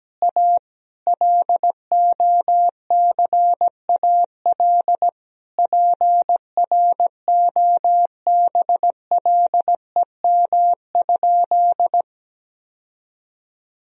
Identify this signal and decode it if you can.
Morse code, heard as a keyed tone, about 17 wpm, 700 Hz. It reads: A LOCAL PROBLEM?
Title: Sentences from Top 1000 Words - Review Part 3 - 17wpm